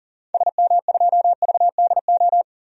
Morse code, heard as a keyed tone, about 40 wpm, 700 Hz. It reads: SM2VDO
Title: DXCC-100 Call Signs - 6 Characters - 40wpm